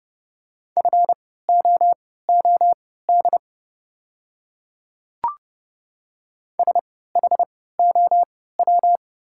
Morse code, heard as a keyed tone, about 30 wpm, spaced as for 17 wpm, 690 Hz. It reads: FOOD SHOW
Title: 3x Character Spacing - Top 500 Words - 30wpm